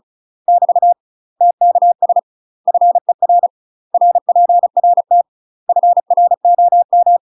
Morse code, heard as a keyed tone, about 36 wpm, 700 Hz. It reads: <BT> TKS FER RPRT FROM
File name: UK MASTER QSOs TRAINING/35WPM